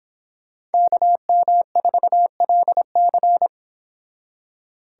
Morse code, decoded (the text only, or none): KM4LC